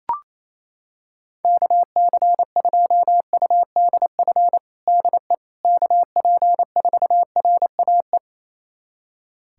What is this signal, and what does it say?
Morse code, heard as a keyed tone, about 28 wpm, 700 Hz. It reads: KC2UDF DE KP4RAE
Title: USA Callsign DE Callsign - 6 Characters - 28wpm